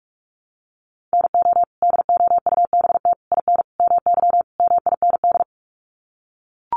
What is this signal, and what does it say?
Morse code, heard as a keyed tone, about 44 wpm, 700 Hz. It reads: NO DOUBT IN MY MIND
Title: Sets of 5 Words Encore2 45wpm